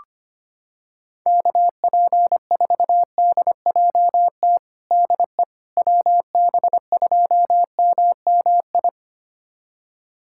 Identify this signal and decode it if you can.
Morse code, heard as a keyed tone, about 25 wpm, 700 Hz. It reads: KP4DJT DE WB2MMI
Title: USA Callsign DE Callsign - 6 Characters - 25wpm